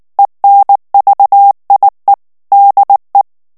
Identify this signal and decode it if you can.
Morse code, heard as a keyed tone, about 19 wpm, 800 Hz. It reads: ENVIE DE
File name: FRANCAIS TEXTE CLAIR 2 VITESSE 18 MOTS MINUTES